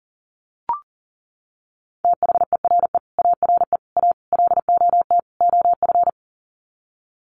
Morse code, heard as a keyed tone, about 40 wpm, 700 Hz. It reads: THERE ARE A LOT OF
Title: Sets of 5 Words 40wpm